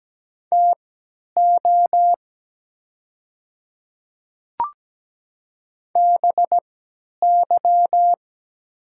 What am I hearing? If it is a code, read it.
Morse code, heard as a keyed tone, about 17 wpm, spaced as for 10 wpm, 700 Hz. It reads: TO BY
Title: 3x Character Spacing - Top 100 Words - 17wpm